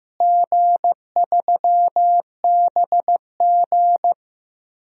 Morse code, heard as a keyed tone, about 15 wpm, 700 Hz. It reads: G3BG
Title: DXCC-100 Call Signs - 4 Characters - Repeated 2x - 15wpm